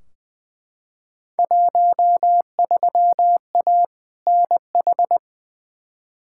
Morse code, heard as a keyed tone, about 20 wpm, 700 Hz. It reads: 13A NH
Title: Field Day Exchanges Encore - 20wpm